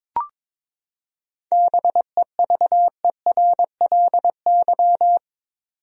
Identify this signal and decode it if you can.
Morse code, heard as a keyed tone, about 22 wpm, 700 Hz. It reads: BEVERLY
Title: USA Common Names - 22wpm